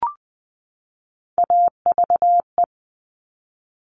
Morse code, heard as a keyed tone, about 20 wpm, 700 Hz. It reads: AVE
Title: Top Three Letter Combinations in Words 20wpm